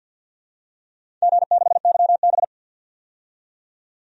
Morse code, heard as a keyed tone, about 49 wpm, 700 Hz. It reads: G6YB
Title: DXCC-100 Call Signs - 4 Characters - 50wpm